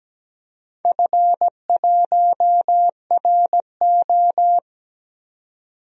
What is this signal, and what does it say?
Morse code, heard as a keyed tone, about 17 wpm, 700 Hz. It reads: F1RO
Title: DXCC-100 Call Signs - 4 Characters - 17wpm